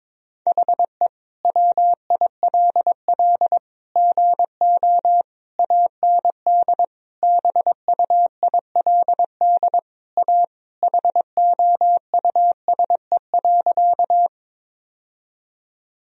Morse code, heard as a keyed tone, about 22 wpm, 700 Hz. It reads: HE WILL GO AND BUILD A HOUSE.
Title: Sentences from Top 1000 Words - Review Part 1 - 22wpm